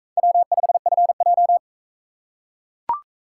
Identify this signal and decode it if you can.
Morse code, heard as a keyed tone, about 39 wpm, 700 Hz. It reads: W5FJ